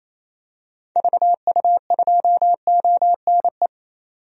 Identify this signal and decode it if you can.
Morse code, heard as a keyed tone, about 28 wpm, 700 Hz. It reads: VU2ONE
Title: DXCC-100 Call Signs - 6 Characters - 28wpm